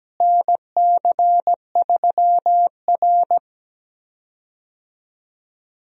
Morse code, heard as a keyed tone, about 17 wpm, 700 Hz. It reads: NC3R
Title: USA Call Signs - 4 Characters - Repeated 2x - 17wpm